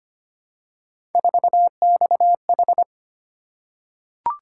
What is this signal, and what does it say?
Morse code, heard as a keyed tone, about 25 wpm, 700 Hz. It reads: 4XH
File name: Three Letter-Number Combinations - 25wpm